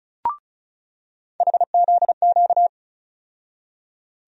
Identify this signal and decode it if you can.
Morse code, heard as a keyed tone, about 36 wpm, 700 Hz. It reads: HZQ
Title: USA Call Sign Postfixes - 3 Letters 35wpm